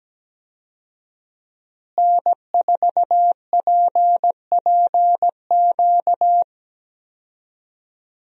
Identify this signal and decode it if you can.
Morse code, heard as a keyed tone, about 17 wpm, 700 Hz. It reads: N4PPQ
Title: USA Call Signs - 5 Characters - Repeated 3x - 17wpm